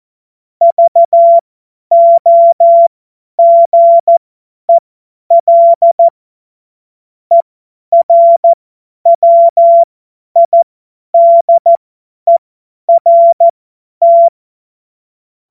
Morse code, heard as a keyed tone, about 14 wpm, spaced as for 10 wpm, 680 Hz. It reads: VOGEL ERWIDERT
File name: Morsekurs / Witze / Witz 04. Juli 2023 50 F70